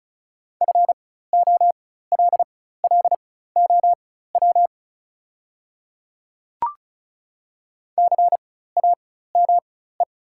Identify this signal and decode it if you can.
Morse code, heard as a keyed tone, about 36 wpm, spaced as for 16 wpm, 700 Hz. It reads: FOLLOW CAME
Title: 4x Character Spacing - Top 200 Words - 35wpm